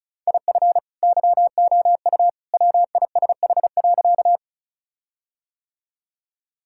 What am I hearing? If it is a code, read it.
Morse code, heard as a keyed tone, about 36 wpm, 700 Hz. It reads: IF YOU WISH.
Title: Sentences from Top 1600 Words 35wpm